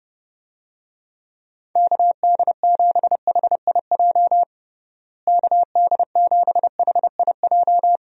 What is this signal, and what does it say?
Morse code, heard as a keyed tone, about 30 wpm, 700 Hz. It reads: KD7HIJ KD7HIJ
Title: USA Call Signs - 6 Characters - Repeated 2x - 30wpm